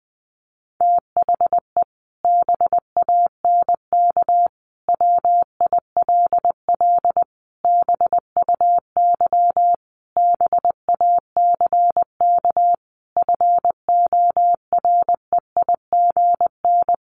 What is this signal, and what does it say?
Morse code, heard as a keyed tone, about 20 wpm, 700 Hz. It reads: THE BANK WILL BUY BACK FOREIGN
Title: Sentences from Top 1500 Words 20wpm